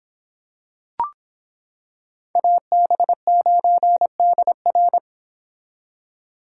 Morse code, heard as a keyed tone, about 25 wpm, 700 Hz. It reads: AB9DR